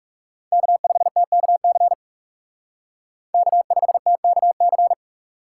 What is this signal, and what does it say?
Morse code, heard as a keyed tone, about 39 wpm, 700 Hz. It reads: K5TKC K5TKC